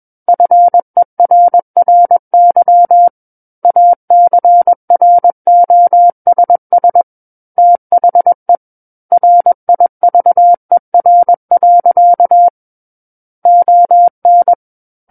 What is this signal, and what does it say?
Morse code, heard as a keyed tone, about 21 wpm, 700 Hz. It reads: FERRY ACROSS THE RIVER. ON
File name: Book One / 12-Chapter 12 - 21 WPM